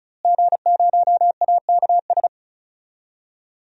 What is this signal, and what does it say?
Morse code, heard as a keyed tone, about 36 wpm, 700 Hz. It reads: G0AKS
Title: DXCC-100 Call Signs - 5 Characters - Repeated 2x - 35wpm